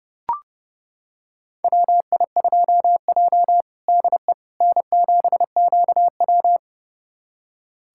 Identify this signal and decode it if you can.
Morse code, heard as a keyed tone, about 30 wpm, 700 Hz. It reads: WI2J DE N7QW
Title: USA Callsign DE Callsign - 4 Characters - 30wpm